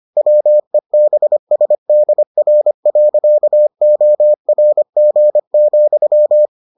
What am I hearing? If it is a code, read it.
Morse code, heard as a keyed tone, about 25 wpm, 600 Hz. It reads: WEBSDR.ORG,